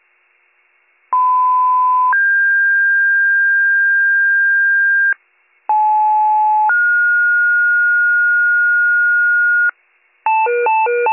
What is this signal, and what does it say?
Two-tone page, heard as a one-second tone, then a three-second tone, about 1010 then 1600 Hz, about 860 then 1430 Hz, with noise. It is heard from 1.1 to 9.7 s.